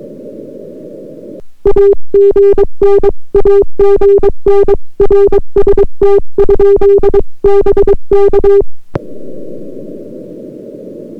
Morse code, heard as a keyed tone, about 22 wpm, 400 Hz. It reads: AGN AGN RST? BK